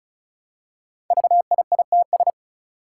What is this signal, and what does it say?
Morse code, heard as a keyed tone, about 36 wpm, 700 Hz. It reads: VIITS
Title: Five Challenging Letter-Number Combinations 35wpm